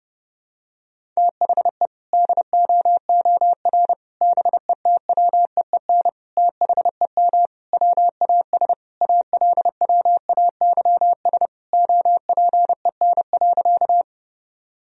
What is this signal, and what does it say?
Morse code, heard as a keyed tone, about 30 wpm, 700 Hz. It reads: THE DOOR BETWEEN THEM WAS ALWAYS OPEN.
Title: Sentences from Top 1000 Words - Review Part 1 - 30wpm